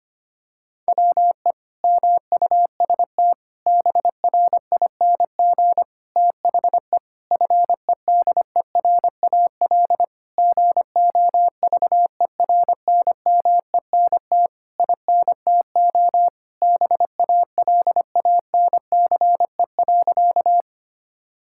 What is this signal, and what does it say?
Morse code, heard as a keyed tone, about 25 wpm, 700 Hz. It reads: WE MUST BRING THE FEDERAL GOVERNMENT INTO BALANCE.
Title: Sentences from Top 1000 Words - Review Part 6 - 25wpm